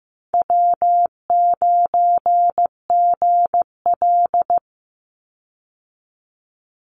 Morse code, heard as a keyed tone, about 15 wpm, 700 Hz. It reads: W9GL
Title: Easy Call Signs Repeated 3x - 15wpm